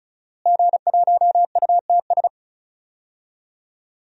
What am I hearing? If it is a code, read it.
Morse code, heard as a keyed tone, about 36 wpm, 700 Hz. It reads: G1UTS
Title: DXCC-100 Call Signs - 5 Characters - Repeated 2x - 35wpm